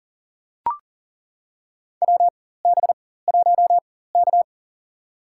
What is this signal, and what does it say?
Morse code, heard as a keyed tone, about 40 wpm, spaced as for 19 wpm, 700 Hz. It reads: WB1K
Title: USA Call Signs - 4 Characters - 4x Character Spacing - 40wpm